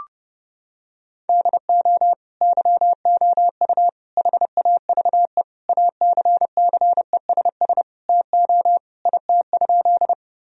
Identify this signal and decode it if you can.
Morse code, heard as a keyed tone, about 32 wpm, 700 Hz. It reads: DO YOU HAVE ACCESS TO IT?
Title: Sentences from Top 1000 Words - Review Part 4 - 30wpm